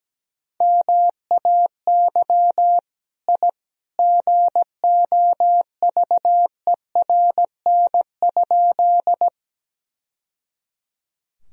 Morse code, heard as a keyed tone, about 17 wpm, 700 Hz. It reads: MAY I GOVERN?
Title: Sentences from Top 1000 Words - Review Part 2 - 17wpm